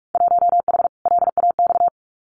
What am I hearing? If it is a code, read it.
Morse code, heard as a keyed tone, about 44 wpm, 700 Hz. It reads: 1H LAX
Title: Winter Field Day Exchanges 45wpm